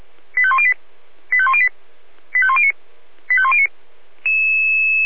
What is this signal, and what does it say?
Five-tone selective call (ZVEI), heard as quick runs of five tones, tones of about 70 ms. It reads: 85108 85108 85109 85109